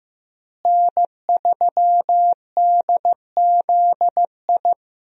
Morse code, heard as a keyed tone, about 16 wpm, 700 Hz. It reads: N3DZI
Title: USA Call Signs - 5 Characters - Repeated 2x - 15wpm